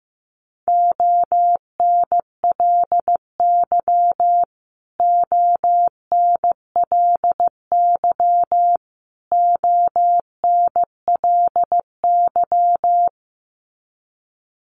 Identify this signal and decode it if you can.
Morse code, heard as a keyed tone, about 15 wpm, 700 Hz. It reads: ONLY ONLY ONLY
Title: Top 500 Words Repeated 15wpm